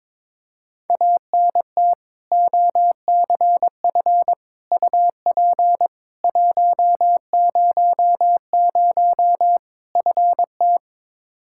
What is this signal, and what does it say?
Morse code, heard as a keyed tone, about 22 wpm, 700 Hz. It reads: ANT OCF UP 100 FT
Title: K1USN SST QSO Practice 22wpm